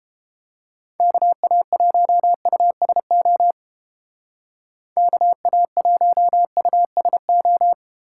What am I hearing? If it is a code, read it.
Morse code, heard as a keyed tone, about 30 wpm, 700 Hz. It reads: KA1USO KA1USO